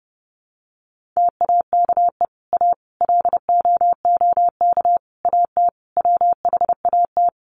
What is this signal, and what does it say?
Morse code, heard as a keyed tone, about 30 wpm, 700 Hz. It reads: TAKE A LOOK AT WHAT
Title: Sets of 5 Words Encore2 30wpm